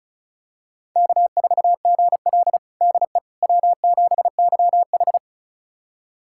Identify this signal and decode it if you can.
Morse code, heard as a keyed tone, about 36 wpm, 700 Hz. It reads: K4GL DE W7YH
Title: USA Callsign DE Callsign - 4 Characters - 35wpm